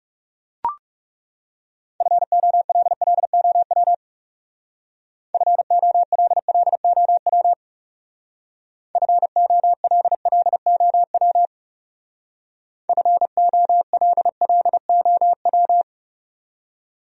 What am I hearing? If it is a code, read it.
Morse code, heard as a keyed tone, about 37 wpm, 700 Hz. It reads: FOLLOW FOLLOW FOLLOW FOLLOW
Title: Speed-Racing Top 300 Words 30wpm